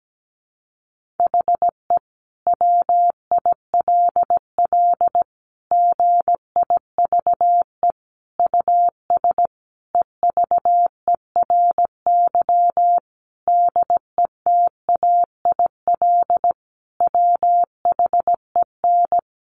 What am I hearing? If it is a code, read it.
Morse code, heard as a keyed tone, about 17 wpm, 700 Hz. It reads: HE WILL GIVE US EVERY DETAIL WHEN
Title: Sentences from Top 1500 Words 17wpm